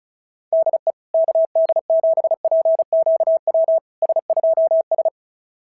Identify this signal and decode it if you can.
Morse code, heard as a keyed tone, about 36 wpm, 690 Hz. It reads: DE KD7PQW S2S
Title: SOTA QSO Practice 35wpm